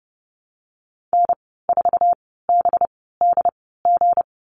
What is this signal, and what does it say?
Morse code, heard as a keyed tone, about 30 wpm, spaced as for 17 wpm, 700 Hz. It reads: N4BDG
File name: Lesson 37g - Letter x - Call Signs 30wpm